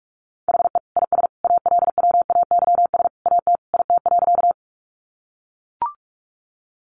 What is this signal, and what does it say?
Morse code, heard as a keyed tone, about 44 wpm, 700 Hz. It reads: HE IS ALWAYS AT IT.
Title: Sentences from Top 300 Words 45wpm